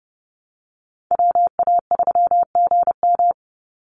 Morse code, heard as a keyed tone, about 30 wpm, 700 Hz. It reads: WA3GM
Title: Call Signs Repeated 2x - 30wpm